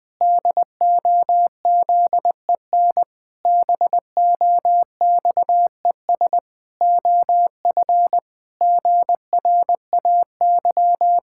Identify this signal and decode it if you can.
Morse code, heard as a keyed tone, about 21 wpm, 700 Hz. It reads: DOZEN BOXES OF GRAY
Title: Pangram Challenge v2 - 20wpm